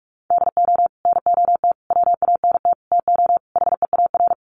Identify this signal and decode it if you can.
Morse code, heard as a keyed tone, about 46 wpm, 700 Hz. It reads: DO NOT WANT TO HEAR
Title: Sets of 5 Words 45wpm